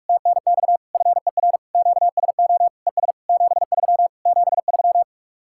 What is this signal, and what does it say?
Morse code, heard as a keyed tone, about 49 wpm, 700 Hz. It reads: TNX FER QSO ES 73 73